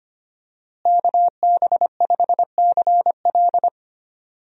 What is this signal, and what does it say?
Morse code, heard as a keyed tone, about 25 wpm, 700 Hz. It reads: KB5CL
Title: USA Call Signs - 5 Characters - Repeated 3x - 25wpm